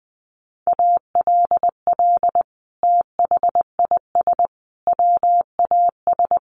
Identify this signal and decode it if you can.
Morse code, heard as a keyed tone, about 20 wpm, 700 Hz. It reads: ALL THIS WAS
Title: Sentences from Top 1500 Words 20wpm